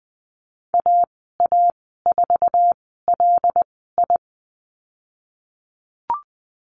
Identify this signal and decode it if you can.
Morse code, heard as a keyed tone, about 20 wpm, spaced as for 15 wpm, 700 Hz. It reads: AA4LI
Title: 082 Lesson 14g - Number 2 - Call Signs 20wpm